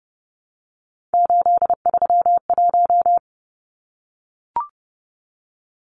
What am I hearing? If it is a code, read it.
Morse code, heard as a keyed tone, about 30 wpm, 700 Hz. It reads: 831
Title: Numbers - 3-Digit 30wpm